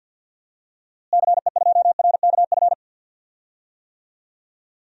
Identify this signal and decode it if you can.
Morse code, heard as a keyed tone, about 50 wpm, 700 Hz. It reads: CE2RKF
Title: DXCC-100 Call Signs - 6 Characters - Repeated 2x - 50wpm